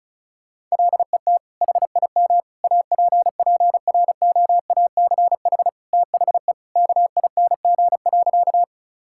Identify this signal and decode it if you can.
Morse code, heard as a keyed tone, about 36 wpm, 700 Hz. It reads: LET HIM APPROACH THE KING.
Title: Sentences from Top 1000 Words - Review Part 6 - 35wpm